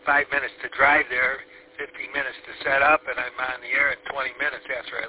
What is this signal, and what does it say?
Emergency Alert System (SAME) tones, heard as harsh buzzing bursts, with speech. It is heard from 1.0 to 4.0 s.